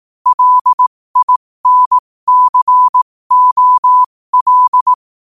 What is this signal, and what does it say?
Morse code, heard as a keyed tone, about 18 wpm, spaced as for 15 wpm, 1000 Hz. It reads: LINCOL
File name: 15wpm